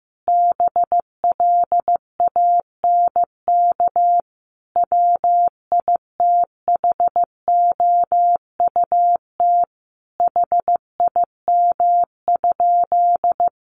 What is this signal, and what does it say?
Morse code, heard as a keyed tone, about 15 wpm, 700 Hz. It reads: BLANK WITHOUT HIM?
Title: Sentences from Top 1500 Words 15wpm